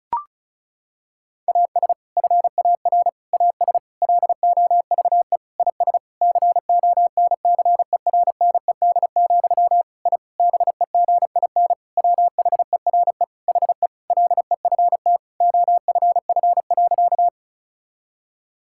Morse code, heard as a keyed tone, about 36 wpm, 700 Hz. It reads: AS FAR AS LOVE IS CONCERNED, I BEGIN WHERE HE LEFT OFF.
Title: Sentences from Top 1000 Words - Review Part 6 - 35wpm